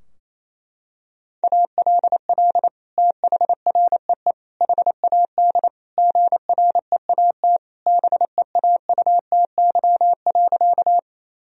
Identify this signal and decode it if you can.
Morse code, heard as a keyed tone, about 28 wpm, 700 Hz. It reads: ALL THREE HAD GREAT BEAUTY.
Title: Sentences from Top 1000 Words - Review Part 2 - 28wpm